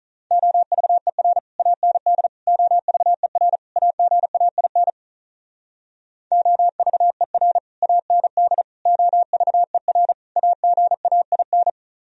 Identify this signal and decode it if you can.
Morse code, heard as a keyed tone, about 45 wpm, 700 Hz. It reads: OVER AND OVER AGAIN OVER AND OVER AGAIN